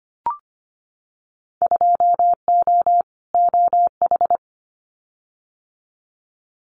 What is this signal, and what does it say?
Morse code, heard as a keyed tone, about 25 wpm, 690 Hz. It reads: 2O OH
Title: Winter Field Day Exchanges 25wpm